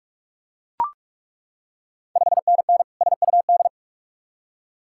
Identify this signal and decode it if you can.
Morse code, heard as a keyed tone, about 41 wpm, 690 Hz. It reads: 5NN SUD
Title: Ontario QSO Party Exchange 45wpm